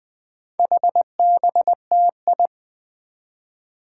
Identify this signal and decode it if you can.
Morse code, heard as a keyed tone, about 20 wpm, 700 Hz. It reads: HBTI
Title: Four Challenging Letter-Number Combinations 20wpm